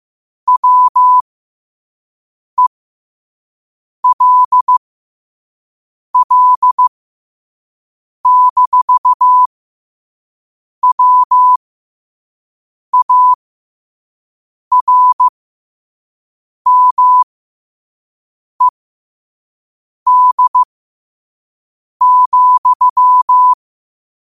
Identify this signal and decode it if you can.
Morse code, heard as a keyed tone, about 15 wpm, spaced as for 5 wpm, 1000 Hz. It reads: WELL-WARMED,
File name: qotdfiles / 5wpm